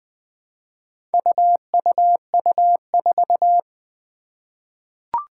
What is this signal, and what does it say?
Morse code, heard as a keyed tone, about 20 wpm, 700 Hz. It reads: UUU4
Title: Four Challenging Letter-Number Combinations 20wpm